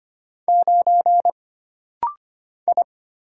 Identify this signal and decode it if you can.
Morse code, heard as a keyed tone, about 25 wpm, 700 Hz. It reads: 9 I
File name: Single Character (Letter:Number:Punctuation:Prosign) - Rapid-Fire - 25wpm